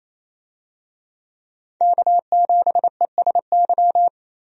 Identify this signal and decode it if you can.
Morse code, heard as a keyed tone, about 28 wpm, 700 Hz. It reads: K7ESY
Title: USA Call Signs - 5 Characters - Repeated 2x - 28wpm